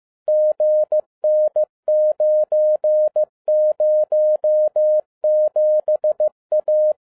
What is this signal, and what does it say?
Morse code, heard as a keyed tone, about 15 wpm, 610 Hz. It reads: GN907A